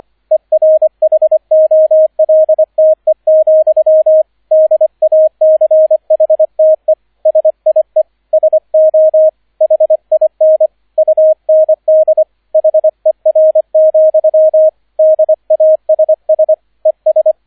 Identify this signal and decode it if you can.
Morse code, heard as a keyed tone, about 25 wpm, 630 Hz, with radio noise. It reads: ERHOLTE, DACHTE SIE SO HIN UND HER, DASS ES